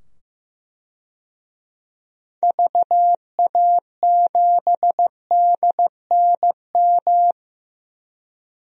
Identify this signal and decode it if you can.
Morse code, heard as a keyed tone, about 15 wpm, 700 Hz. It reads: VA7DNM
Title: DXCC-100 Call Signs - 6 Characters - 15wpm